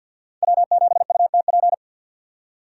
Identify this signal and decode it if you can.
Morse code, heard as a keyed tone, about 64 wpm, 700 Hz. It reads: W7UTP